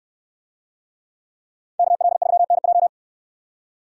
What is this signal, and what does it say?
Morse code, heard as a keyed tone, about 69 wpm, 700 Hz. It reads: KL3NJ